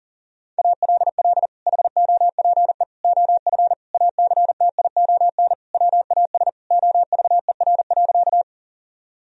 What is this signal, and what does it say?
Morse code, heard as a keyed tone, about 40 wpm, 700 Hz. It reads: ALL HOPE OF ACTION WAS OVER.
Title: Sentences from Top 1000 Words - Review Part 4 - 40wpm